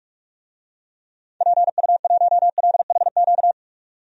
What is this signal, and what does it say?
Morse code, heard as a keyed tone, about 44 wpm, 700 Hz. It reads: PU1LHQ
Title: DXCC-100 Call Signs - 6 Characters - 45wpm